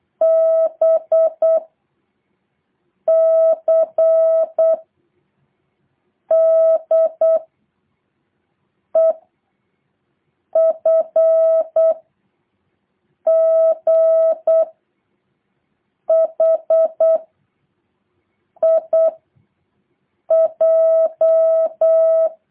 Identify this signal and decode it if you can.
Morse code, heard as a keyed tone, about 8 wpm, 650 Hz. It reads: B C D E F G H I J
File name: jh8cbh / 2023cw